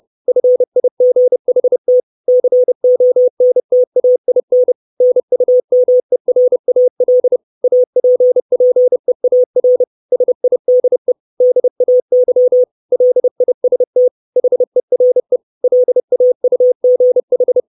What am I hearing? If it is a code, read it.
Morse code, heard as a keyed tone, about 30 wpm, 500 Hz. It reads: FIGHT CONTAIN NUMERAL APPEAR SIDE DAY LIST HERE LAUGH
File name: 632 Words 30WPM